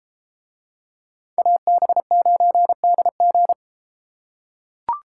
Morse code, heard as a keyed tone, about 33 wpm, 700 Hz. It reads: AB9DG